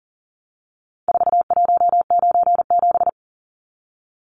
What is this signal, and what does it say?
Morse code, heard as a keyed tone, about 40 wpm, 700 Hz. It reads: 4197
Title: Numbers - 4-Digit 40wpm